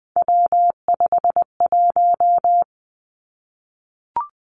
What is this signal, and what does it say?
Morse code, heard as a keyed tone, about 20 wpm, 700 Hz. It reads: W51
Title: Three Challenging Letter-Number Combinations 20wpm